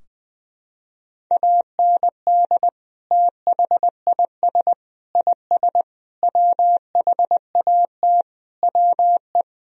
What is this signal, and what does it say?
Morse code, heard as a keyed tone, about 21 wpm, 700 Hz. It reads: AND THIS IS WHAT WE
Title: Sentences from Top 1000 Words - Review Part 3 - 20wpm